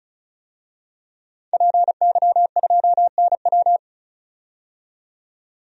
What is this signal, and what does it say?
Morse code, heard as a keyed tone, about 36 wpm, 700 Hz. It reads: PY2NW
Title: DXCC-100 Call Signs - 5 Characters - 35wpm